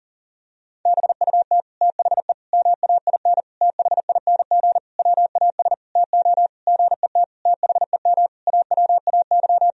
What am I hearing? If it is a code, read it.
Morse code, heard as a keyed tone, about 40 wpm, 700 Hz. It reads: BUT THE MAIN THING WAS TO GET THEM AWAY
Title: Sentences from Top 1000 Words - Review Part 2 - 40wpm